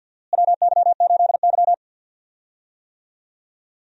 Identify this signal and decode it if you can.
Morse code, heard as a keyed tone, about 52 wpm, 700 Hz. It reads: WY8Y
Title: USA Call Signs - 4 Characters - Repeated 3x - 50wpm